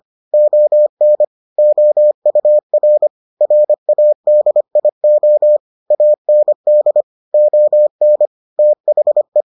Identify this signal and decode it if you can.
Morse code, heard as a keyed tone, about 25 wpm, 600 Hz. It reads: ON OUR RADIO AND ON THE